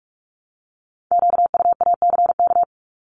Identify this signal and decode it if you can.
Morse code, heard as a keyed tone, about 46 wpm, 700 Hz. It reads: QUACK
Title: Multi-letter Phonemes Repeated 3x 45wpm